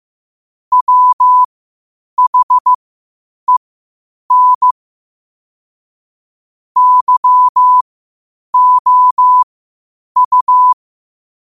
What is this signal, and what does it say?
Morse code, heard as a keyed tone, about 15 wpm, spaced as for 8 wpm, 1000 Hz. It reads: WHEN YOU
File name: qotdfiles / 8wpm